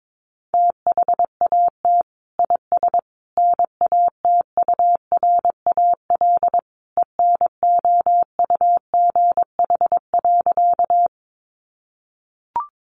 Morse code, heard as a keyed tone, about 22 wpm, 700 Hz. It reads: THAT IS NATURAL ENOUGH.
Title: Sentences from Top 700 Words 22wpm